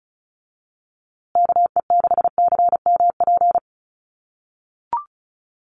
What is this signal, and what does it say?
Morse code, heard as a keyed tone, about 34 wpm, 700 Hz. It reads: KE6CMP
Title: USA Call Signs Repeated 3x 35wpm